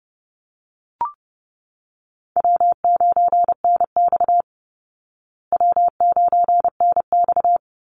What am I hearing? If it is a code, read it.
Morse code, heard as a keyed tone, about 30 wpm, 700 Hz. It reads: W9NX W9NX
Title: Easy Call Signs Repeated 2x - 30wpm